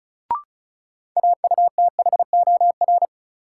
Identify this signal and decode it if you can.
Morse code, heard as a keyed tone, about 36 wpm, 700 Hz. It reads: AUTHOR
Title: Sentences from Top 1200 Words - New Word Drill Alt - 35wpm